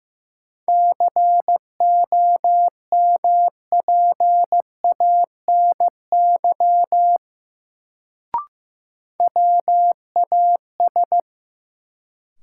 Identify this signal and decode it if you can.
Morse code, heard as a keyed tone, about 15 wpm, 700 Hz. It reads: COMPANY WAS
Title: Sets of 2 Words - New Word Drill - 15wpm